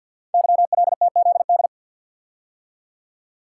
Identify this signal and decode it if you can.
Morse code, heard as a keyed tone, about 51 wpm, 700 Hz. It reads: YLTZD
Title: Code Groups - 5 Letter:Numbers - 50wpm